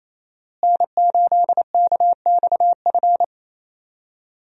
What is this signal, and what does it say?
Morse code, heard as a keyed tone, about 27 wpm, 700 Hz. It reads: N8KXF
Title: USA Call Signs - 5 Characters - Repeated 3x - 28wpm